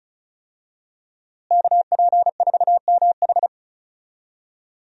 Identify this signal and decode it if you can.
Morse code, heard as a keyed tone, about 34 wpm, 700 Hz. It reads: KP4MH
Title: DXCC-100 Call Signs - 5 Characters - Repeated 2x - 35wpm